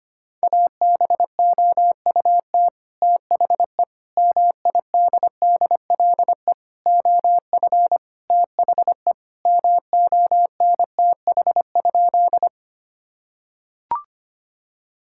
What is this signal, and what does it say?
Morse code, heard as a keyed tone, about 25 wpm, 700 Hz. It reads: ABOUT THE MIDDLE OF THE MONTH?
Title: Sentences from Top 1000 Words - Review Part 6 - 25wpm